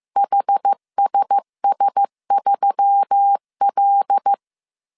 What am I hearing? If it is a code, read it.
Morse code, heard as a keyed tone, about 15 wpm, 800 Hz. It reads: HSS3L
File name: CW / DET / 1 6 1500C S B 14wpm 2958